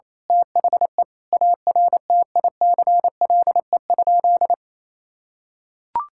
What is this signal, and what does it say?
Morse code, heard as a keyed tone, about 28 wpm, 700 Hz. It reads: THE ARTICLE?
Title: Sentences from Top 1000 Words - Review Part 5 - 28wpm